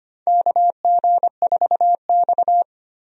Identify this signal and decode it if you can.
Morse code, heard as a keyed tone, about 25 wpm, 700 Hz. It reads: KG4X
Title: USA Call Signs - 4 Characters - 25wpm